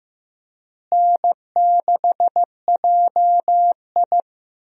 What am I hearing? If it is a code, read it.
Morse code, heard as a keyed tone, about 15 wpm, 700 Hz. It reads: N6JI
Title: USA Call Signs - 4 Characters - Repeated 3x - 15wpm